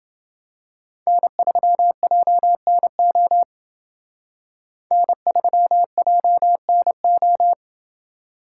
Morse code, heard as a keyed tone, about 28 wpm, 700 Hz. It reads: N3JNO N3JNO